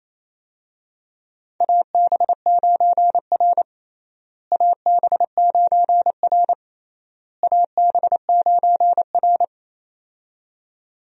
Morse code, heard as a keyed tone, about 28 wpm, 700 Hz. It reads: AB9R AB9R AB9R
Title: USA Call Signs - 4 Characters - Repeated 3x - 28wpm